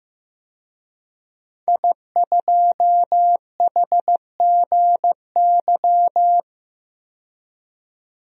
Morse code, heard as a keyed tone, about 15 wpm, 700 Hz. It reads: I2HGY
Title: DXCC-100 Call Signs - 5 Characters - 15wpm